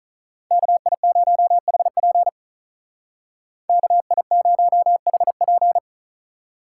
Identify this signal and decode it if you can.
Morse code, heard as a keyed tone, about 43 wpm, 700 Hz. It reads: KI0HP KI0HP
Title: Speed Racing - USA 2x2 Call Signs 35wpm